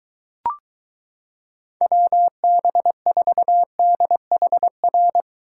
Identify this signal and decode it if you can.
Morse code, heard as a keyed tone, about 22 wpm, 700 Hz. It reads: WB4DHR